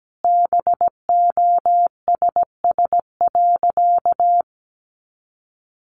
Morse code, heard as a keyed tone, about 17 wpm, 700 Hz. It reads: BOSS.
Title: Sentences from Top 700 Words 17wpm